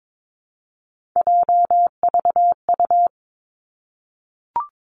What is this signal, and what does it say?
Morse code, heard as a keyed tone, about 22 wpm, 700 Hz. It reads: JVU
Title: Three Challenging Letter-Number Combinations 22wpm